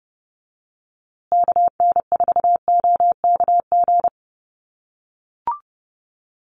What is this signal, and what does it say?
Morse code, heard as a keyed tone, about 30 wpm, 700 Hz. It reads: KN4OKG
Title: USA Call Signs Repeated 3x 30wpm